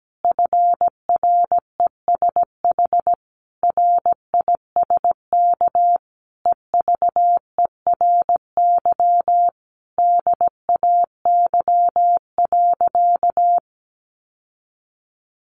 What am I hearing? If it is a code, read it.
Morse code, heard as a keyed tone, about 17 wpm, 700 Hz. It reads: FRESH RISK EVERY DAY.
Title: Sentences from Top 1200 Words 17wpm